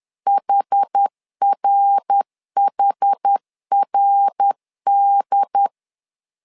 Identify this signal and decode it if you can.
Morse code, heard as a keyed tone, about 10 wpm, 800 Hz. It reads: HRHRD
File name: LEZIONI / 4 Q 10wpm 2210